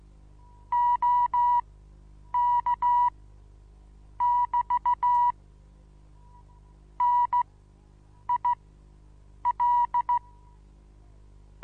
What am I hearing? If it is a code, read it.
Morse code, heard as a keyed tone, about 16 wpm, spaced as for 7 wpm, 970 Hz. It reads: OK<BT> NIL
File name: Onlinelehrgang / morsen / DJ4UF Morsen25